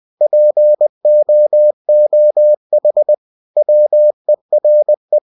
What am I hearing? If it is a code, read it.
Morse code, heard as a keyed tone, about 20 wpm, 600 Hz. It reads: POOH WERE
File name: cw-books / winnie-the-pooh / winnie-20wpm-page-0006